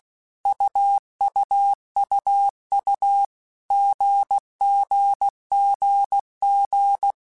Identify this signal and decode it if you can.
Morse code, heard as a keyed tone, about 16 wpm, 800 Hz. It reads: UUUU GGGG
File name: Basic Letter Groups By7Groups / AWUNDG / AWUNDG@15